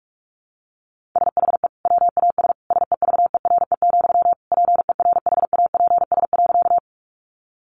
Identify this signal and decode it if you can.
Morse code, heard as a keyed tone, about 44 wpm, 700 Hz. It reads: SHE WAS SEVERE, PERHAPS.
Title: Sentences from Top 1400 Words 45wpm